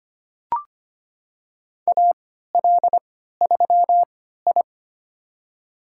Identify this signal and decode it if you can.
Morse code, heard as a keyed tone, about 25 wpm, spaced as for 14 wpm, 700 Hz. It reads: AL3I
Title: USA Call Signs - 4 Characters - 3x Character Spacing - 25wpm